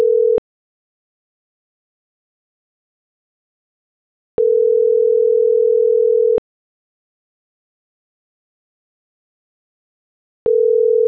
Ringback tone (North America), 440 + 480 Hz.